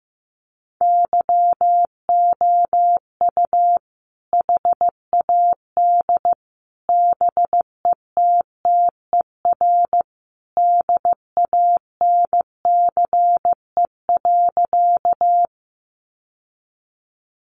Morse code, heard as a keyed tone, about 15 wpm, 700 Hz. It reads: YOU HAD BETTER DANCE.
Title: Sentences from Top 700 Words 15wpm